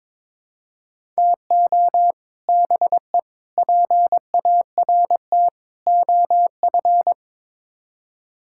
Morse code, heard as a keyed tone, about 22 wpm, 700 Hz. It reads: TO BE PART OF
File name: practice / 22wpm / Build A Phrase - 5-Word Phrases - 22wpm